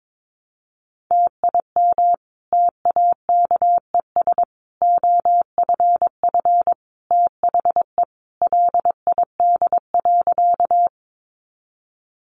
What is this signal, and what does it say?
Morse code, heard as a keyed tone, about 22 wpm, 700 Hz. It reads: TIM TAKES OFF THE LID.
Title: Easy Phrases 22wpm